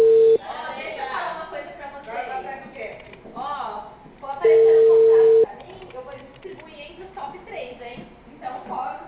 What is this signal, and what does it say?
Ringback tone (China, 450 Hz), with speech.